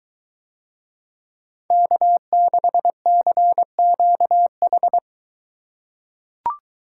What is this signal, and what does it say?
Morse code, heard as a keyed tone, about 22 wpm, 700 Hz. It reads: K6CQH